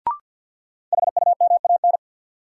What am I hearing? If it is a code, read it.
Morse code, heard as a keyed tone, about 47 wpm, 700 Hz. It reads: HUMAN
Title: Sets of 5 Words - New Word Drill - 50wpm